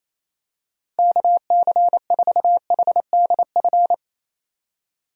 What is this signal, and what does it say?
Morse code, heard as a keyed tone, about 28 wpm, 700 Hz. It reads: KC4HDF
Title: USA Call Signs - 6 Characters - Repeated 3x - 28wpm